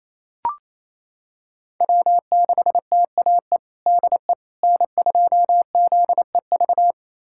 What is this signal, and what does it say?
Morse code, heard as a keyed tone, about 28 wpm, 700 Hz. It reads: W6TAE DE N2ZEV
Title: USA Callsign DE Callsign - 5 Characters - 28wpm